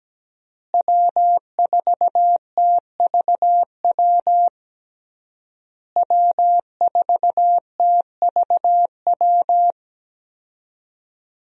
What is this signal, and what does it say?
Morse code, heard as a keyed tone, about 17 wpm, 700 Hz. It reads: W4TVW W4TVW
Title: USA Call Signs - 5 Characters - Repeated 3x - 17wpm